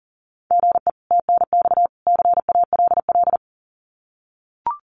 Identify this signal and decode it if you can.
Morse code, heard as a keyed tone, about 40 wpm, 700 Hz. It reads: GE TNX CALL
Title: Common QSO Phrases 40wpm